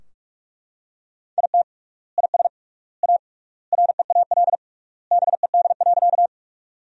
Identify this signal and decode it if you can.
Morse code, heard as a keyed tone, about 43 wpm, 700 Hz. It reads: IT IS A REAL BED.
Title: 3x Word-Spacing - Sentences from Top 500 Words - 45wpm